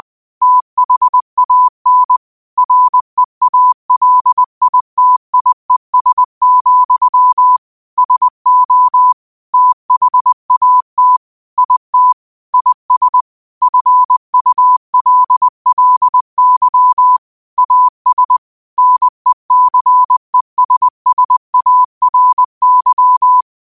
Morse code, heard as a keyed tone, about 20 wpm, 1000 Hz. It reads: THAN REALITIES, SO THAT IT IS FULLY AS NECESSARY
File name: qotdfiles / 20wpm